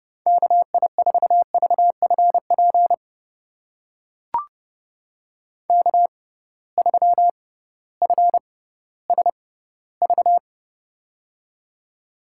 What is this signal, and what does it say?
Morse code, heard as a keyed tone, about 31 wpm, spaced as for 16 wpm, 700 Hz. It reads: KI4VFP K3FSV
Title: USA Call Signs - 6x Character Spacing - 30wpm